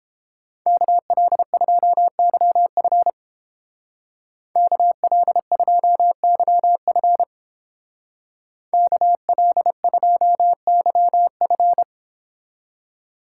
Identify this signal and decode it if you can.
Morse code, heard as a keyed tone, about 30 wpm, 700 Hz. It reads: KL2YF KL2YF KL2YF